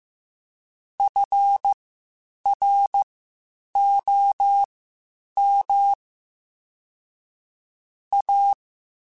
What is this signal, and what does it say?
Morse code, heard as a keyed tone, about 15 wpm, spaced as for 8 wpm, 800 Hz. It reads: FROM A